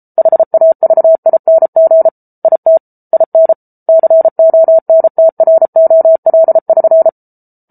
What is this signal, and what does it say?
Morse code, heard as a keyed tone, about 34 wpm, 650 Hz. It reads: HAVING IT IN CONTROL<SN>